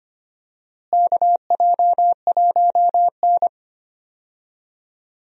Morse code, heard as a keyed tone, about 25 wpm, 700 Hz. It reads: KJ1N
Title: USA Call Signs - 4 Characters - Repeated 3x - 25wpm